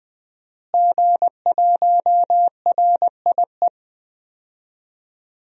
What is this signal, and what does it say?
Morse code, heard as a keyed tone, about 21 wpm, 700 Hz. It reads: G1RIE
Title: DXCC-100 Call Signs - 5 Characters - Repeated 3x - 20wpm